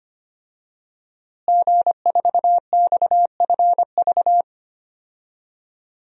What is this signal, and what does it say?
Morse code, heard as a keyed tone, about 25 wpm, 700 Hz. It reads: G4XFV
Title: DXCC-100 Call Signs - 5 Characters - Repeated 3x - 25wpm